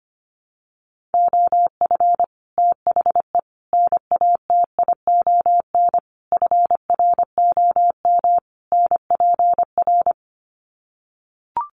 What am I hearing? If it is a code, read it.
Morse code, heard as a keyed tone, about 25 wpm, 700 Hz. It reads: OF THE NATION FROM NPR
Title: Sets of 5 Words 25wpm